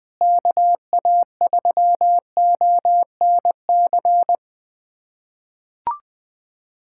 Morse code, heard as a keyed tone, about 21 wpm, 700 Hz. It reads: KA3ONC
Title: USA Callsign DE Callsign 20wpm